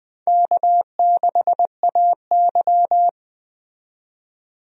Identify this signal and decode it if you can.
Morse code, heard as a keyed tone, about 20 wpm, 700 Hz. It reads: K6AY
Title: USA Call Signs - 4 Characters - 20wpm